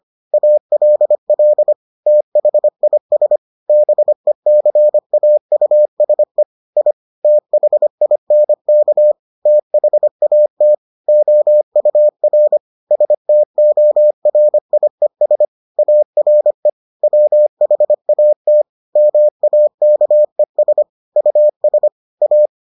Morse code, heard as a keyed tone, about 25 wpm, 600 Hz. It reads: ALL THIS BECAUSE I THINK THAT OUR STORIES ARE WHAT MAKES US A